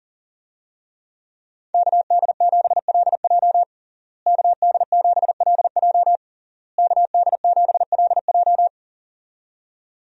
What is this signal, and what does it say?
Morse code, heard as a keyed tone, about 40 wpm, 700 Hz. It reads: KD7LJ KD7LJ KD7LJ
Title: USA Call Signs - 5 Characters - Repeated 3x - 40wpm